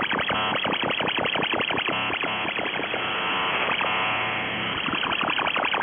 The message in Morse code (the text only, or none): none